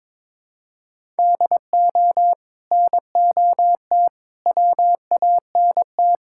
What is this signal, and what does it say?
Morse code, heard as a keyed tone, about 21 wpm, 700 Hz. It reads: DO NOT WANT
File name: practice / 22wpm / Build A Phrase - 5-Word Phrases - 22wpm